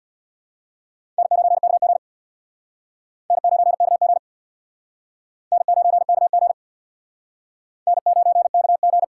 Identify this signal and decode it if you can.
Morse code, heard as a keyed tone, about 67 wpm, 700 Hz. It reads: N9KG N9KG N9KG N9KG